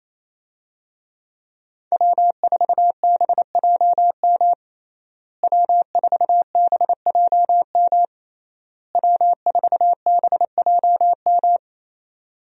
Morse code, heard as a keyed tone, about 27 wpm, 700 Hz. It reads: W4BJM W4BJM W4BJM
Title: USA Call Signs - 5 Characters - Repeated 3x - 28wpm